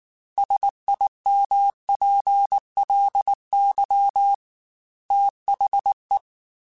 Morse code, heard as a keyed tone, about 19 wpm, 800 Hz. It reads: SIMPLY THE
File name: Technical Subjects / Technical Subjects-mp3s / XC @18